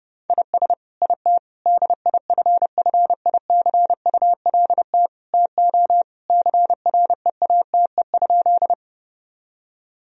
Morse code, heard as a keyed tone, about 30 wpm, 700 Hz. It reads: IS IT DIFFICULT TO CREATE?
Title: Sentences from Top 1000 Words - Review Part 3 - 30wpm